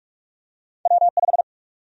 Morse code, heard as a keyed tone, about 43 wpm, 700 Hz. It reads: W5
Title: USA Call Sign Prefixes - 45wpm